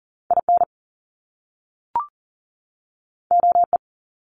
Morse code, heard as a keyed tone, about 41 wpm, 700 Hz. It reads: IN OE
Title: Top Two Letter Combinations in Words 40wpm